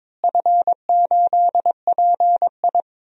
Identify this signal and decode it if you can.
Morse code, heard as a keyed tone, about 22 wpm, 700 Hz. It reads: F8PI
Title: DXCC-100 Call Signs - 4 Characters - Repeated 2x - 22wpm